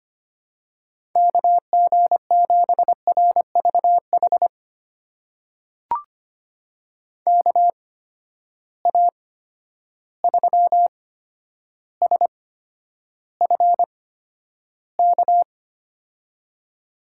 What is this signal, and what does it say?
Morse code, heard as a keyed tone, about 25 wpm, spaced as for 11 wpm, 700 Hz. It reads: KG7RVH KA3SFK
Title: USA Call Signs - 6 Characters - 8x Character Spacing - 25wpm